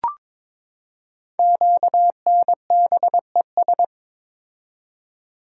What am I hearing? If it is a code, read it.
Morse code, heard as a keyed tone, about 22 wpm, 700 Hz. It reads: QNBES
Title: Code Groups - 5 Letters - 22wpm